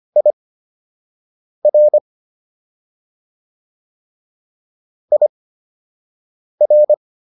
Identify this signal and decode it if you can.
Morse code, heard as a keyed tone, about 25 wpm, spaced as for 6 wpm, 600 Hz. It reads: IR IR